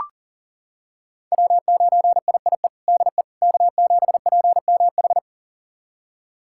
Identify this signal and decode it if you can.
Morse code, heard as a keyed tone, about 40 wpm, 700 Hz. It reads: W9IIE DE K7PMH
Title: USA Callsign DE Callsign - 5 Characters - 40wpm